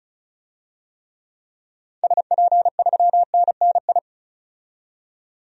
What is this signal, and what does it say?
Morse code, heard as a keyed tone, about 36 wpm, 700 Hz. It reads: SP3NNI
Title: DXCC-100 Call Signs - 6 Characters - 35wpm